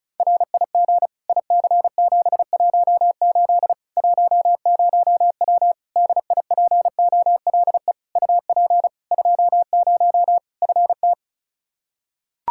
Morse code, heard as a keyed tone, about 36 wpm, 700 Hz. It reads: RIG IC718 10W DIPOLE UP 20 FT
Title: Practice QSO - Exchange 2 - Info - 35wpm